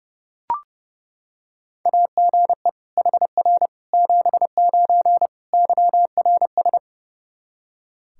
Morse code, heard as a keyed tone, about 32 wpm, 700 Hz. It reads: AGE HR 79 YRS
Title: Practice QSO - Exchange 3 - More Info - 30wpm